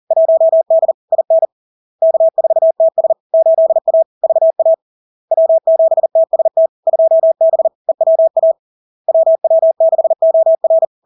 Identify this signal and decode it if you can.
Morse code, heard as a keyed tone, about 40 wpm, 650 Hz. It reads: 1D IN K4TS 8A VA W7TST 2B EWA WW6OR
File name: LARK-FD-2022-Log-in-CW-40wpm-650-Hz